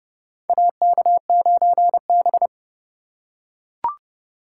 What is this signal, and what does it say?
Morse code, heard as a keyed tone, about 32 wpm, 700 Hz. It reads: AK9B
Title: USA Call Signs - 4 Characters - 2x Character Spacing - 30wpm